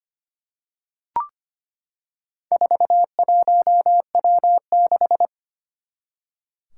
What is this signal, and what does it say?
Morse code, heard as a keyed tone, about 25 wpm, 700 Hz. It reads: 41W6
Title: Four Challenging Letter-Number Combinations 25wpm